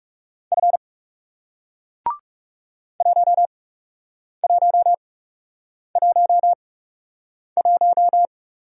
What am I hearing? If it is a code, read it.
Morse code, heard as a keyed tone, about 40 wpm, spaced as for 8 wpm, 700 Hz. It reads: F 1111